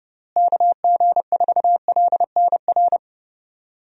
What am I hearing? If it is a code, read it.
Morse code, heard as a keyed tone, about 30 wpm, 700 Hz. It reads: KG4LNR
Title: USA Call Signs - 6 Characters - Repeated 3x - 30wpm